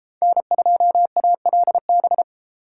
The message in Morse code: N2ALB

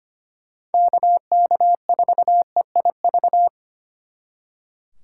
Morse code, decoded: KK4EIV

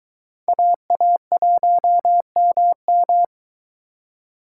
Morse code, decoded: AA1MM

